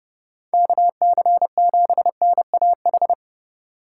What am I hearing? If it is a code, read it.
Morse code, heard as a keyed tone, about 30 wpm, 700 Hz. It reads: KC7NAH